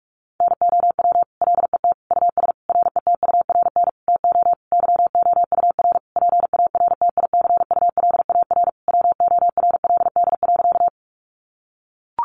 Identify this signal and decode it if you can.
Morse code, heard as a keyed tone, about 44 wpm, 700 Hz. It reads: NOW LET US RETURN TO YOUR PARTICULAR WORLD.
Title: Sentences from Top 800 Words 45wpm